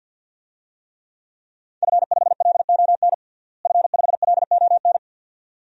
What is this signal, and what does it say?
Morse code, heard as a keyed tone, about 52 wpm, 700 Hz. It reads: F5LON F5LON
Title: DXCC-100 Call Signs - 5 Characters - Repeated 2x - 50wpm